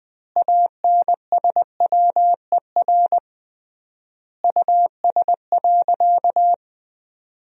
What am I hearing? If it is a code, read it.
Morse code, heard as a keyed tone, about 21 wpm, 700 Hz. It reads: ANSWER US.
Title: 3x Word-Spacing - Sentences from Top 300 Words - 20wpm